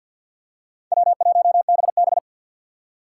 Morse code, heard as a keyed tone, about 52 wpm, 690 Hz. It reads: W1BB
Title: USA Call Signs - 4 Characters - 6x Character Spacing - 50wpm